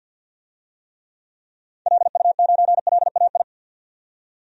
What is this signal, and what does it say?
Morse code, heard as a keyed tone, about 50 wpm, 700 Hz. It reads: LU9LAI
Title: DXCC-100 Call Signs - 6 Characters - 50wpm